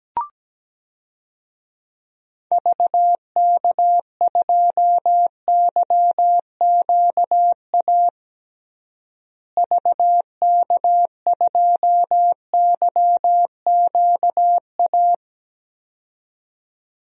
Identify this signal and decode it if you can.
Morse code, heard as a keyed tone, about 17 wpm, 700 Hz. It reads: VK2YQA VK2YQA
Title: DXCC-100 Call Signs - 6 Characters - Repeated 2x - 17wpm